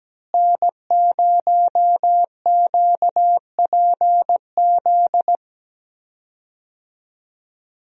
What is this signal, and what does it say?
Morse code, heard as a keyed tone, about 17 wpm, 700 Hz. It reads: N0QPZ
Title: USA Call Signs - 5 Characters - Repeated 3x - 17wpm